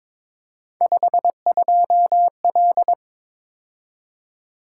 Morse code, heard as a keyed tone, about 26 wpm, 700 Hz. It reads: 52L
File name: Speed-Racing - Three Letter-Number Combinations - 22wpm